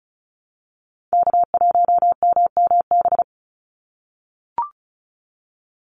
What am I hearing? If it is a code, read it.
Morse code, heard as a keyed tone, about 34 wpm, 700 Hz. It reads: K1MMB
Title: Call Signs - 35wpm